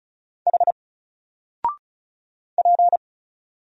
Morse code, heard as a keyed tone, about 36 wpm, 700 Hz. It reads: H P
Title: Single Letter-Number 35wpm